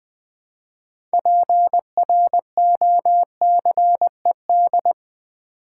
Morse code, heard as a keyed tone, about 20 wpm, 700 Hz. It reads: PROCED
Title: Build A Word - 3 Letters At A Time - 9-Letter Words - 20wpm